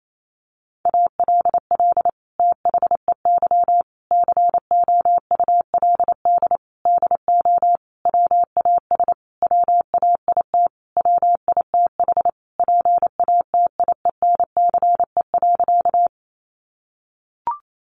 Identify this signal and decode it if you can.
Morse code, heard as a keyed tone, about 28 wpm, 700 Hz. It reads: ALL THEY COULD DO WAS WAIT WITH PATIENCE.
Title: Sentences from Top 1300 Words 28wpm